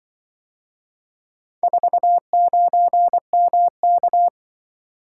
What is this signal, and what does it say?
Morse code, heard as a keyed tone, about 24 wpm, 700 Hz. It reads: 49MK
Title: Speed-Racing - Four Letter-Number Combinations - 20wpm